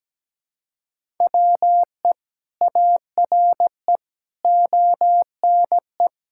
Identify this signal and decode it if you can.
Morse code, heard as a keyed tone, about 17 wpm, 700 Hz. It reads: WE ARE ONE
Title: Sentences from Top 1000 Words - Review Part 6 - 17wpm